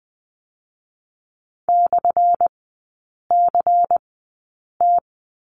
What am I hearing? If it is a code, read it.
Morse code, heard as a keyed tone, about 20 wpm, spaced as for 8 wpm, 700 Hz. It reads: /CT
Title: 178 Lesson 29e - Slash - Three Characters 20wpm